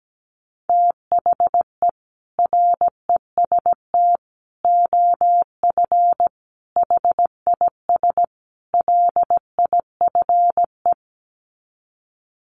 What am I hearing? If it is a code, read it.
Morse code, heard as a keyed tone, about 17 wpm, 700 Hz. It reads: THE REST OF HIS LIFE
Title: Sets of 5 Words 17wpm